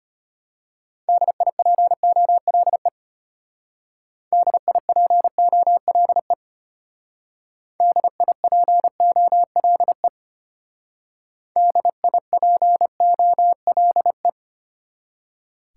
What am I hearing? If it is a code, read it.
Morse code, heard as a keyed tone, about 31 wpm, 700 Hz. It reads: DIPOLE DIPOLE DIPOLE DIPOLE